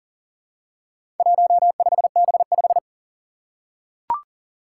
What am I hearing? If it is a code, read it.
Morse code, heard as a keyed tone, about 39 wpm, 700 Hz. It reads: 15B5
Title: Four Challenging Letter-Number Combinations 40wpm